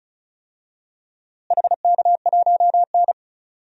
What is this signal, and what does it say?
Morse code, heard as a keyed tone, about 34 wpm, 700 Hz. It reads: HK1N
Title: DXCC-100 Call Signs - 4 Characters - Repeated 3x - 35wpm